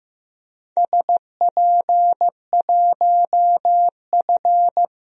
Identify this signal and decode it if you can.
Morse code, heard as a keyed tone, about 15 wpm, 700 Hz. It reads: SP1F
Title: DXCC-100 Call Signs - 4 Characters - Repeated 2x - 15wpm